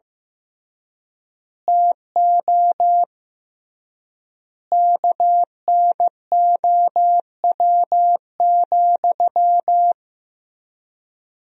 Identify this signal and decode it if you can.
Morse code, heard as a keyed tone, about 15 wpm, 700 Hz. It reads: TO KNOW,
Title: 3x Word-Spacing - Sentences from Top 200 Words - 15wpm